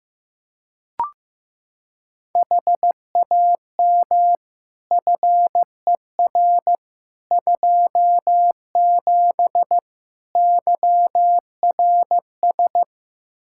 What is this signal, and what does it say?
Morse code, heard as a keyed tone, about 16 wpm, 700 Hz. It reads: HAM FER 27 YRS
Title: Practice QSO - Exchange 3 - More Info - 15wpm